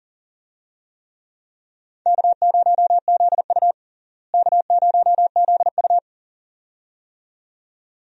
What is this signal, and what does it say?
Morse code, heard as a keyed tone, about 40 wpm, 700 Hz. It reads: K0ZU K0ZU
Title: USA Call Signs - 4 Characters - Repeated 2x - 40wpm